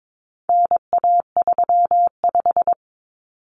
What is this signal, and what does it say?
Morse code, heard as a keyed tone, about 22 wpm, 700 Hz. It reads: NA35
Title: Common Cut Numbers - 4-Digit 22wpm